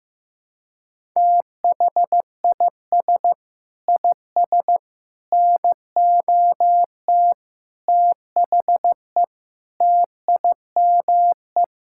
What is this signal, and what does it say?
Morse code, heard as a keyed tone, about 15 wpm, 700 Hz. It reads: THIS IS NOT THE TIME
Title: Sentences from Top 1000 Words - Review Part 5 - 15wpm